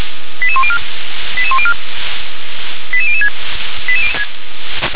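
Five-tone selective call (ZVEI), heard as quick runs of five tones, tones of about 70 ms. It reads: 88104 88104 88006
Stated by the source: ZVEI five-tone selective call 88083, received on SDR.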